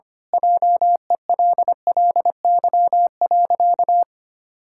Morse code, heard as a keyed tone, about 25 wpm, 700 Hz. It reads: JELLY.
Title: Pangram Challenge v2 - 25wpm